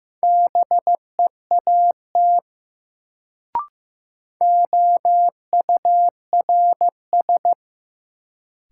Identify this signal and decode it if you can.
Morse code, heard as a keyed tone, about 15 wpm, 700 Hz. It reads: BEAT OURS
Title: Sentences from Top 1300 Words - New Word Drill Alt - 15wpm